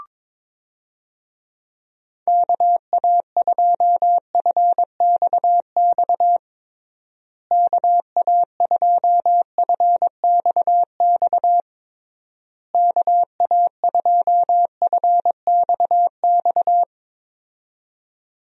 Morse code, heard as a keyed tone, about 22 wpm, 700 Hz. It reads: KA2FXX KA2FXX KA2FXX
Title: USA Call Signs - 6 Characters - Repeated 3x - 22wpm